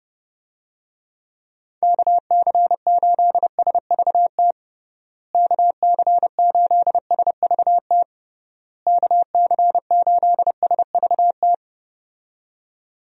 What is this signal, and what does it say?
Morse code, heard as a keyed tone, about 30 wpm, 700 Hz. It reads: KC8SVT KC8SVT KC8SVT
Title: USA Call Signs - 6 Characters - Repeated 3x - 30wpm